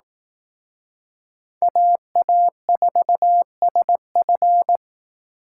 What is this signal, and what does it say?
Morse code, heard as a keyed tone, about 19 wpm, 700 Hz. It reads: AA4SF
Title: Speed Racing - USA 2x2 Call Signs 15wpm